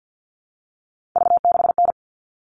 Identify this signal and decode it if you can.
Morse code, heard as a keyed tone, about 50 wpm, 700 Hz. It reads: V6N
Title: All Cut Numbers - 3-Digit 50wpm